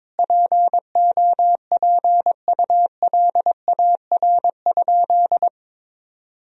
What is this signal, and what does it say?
Morse code, heard as a keyed tone, about 22 wpm, 700 Hz. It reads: POPULAR?
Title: Sentences from Top 1000 Words - Review Part 3 - 22wpm